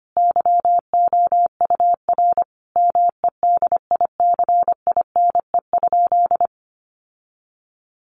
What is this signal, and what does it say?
Morse code, heard as a keyed tone, about 25 wpm, 700 Hz. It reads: YOUR MEDICINE?
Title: Sentences from Top 900 Words 25wpm